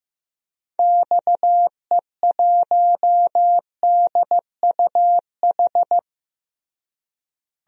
Morse code, heard as a keyed tone, about 15 wpm, 700 Hz. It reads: XE1DUH
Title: DXCC-100 Call Signs - 6 Characters - Repeated 2x - 15wpm